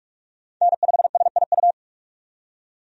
Morse code, heard as a keyed tone, about 43 wpm, 700 Hz. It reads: N5SIU
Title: USA Call Signs - 5 Characters - 45wpm